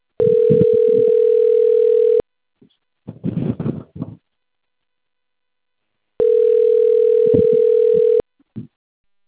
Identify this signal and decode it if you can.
Ringback tone (North America), 440 + 480 Hz.